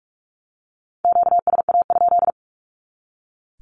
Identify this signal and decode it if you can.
Morse code, heard as a keyed tone, about 44 wpm, 700 Hz. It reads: QSA?
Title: Q-Codes 45wpm